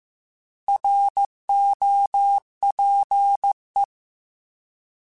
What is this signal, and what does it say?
Morse code, heard as a keyed tone, about 15 wpm, 800 Hz. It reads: ROPE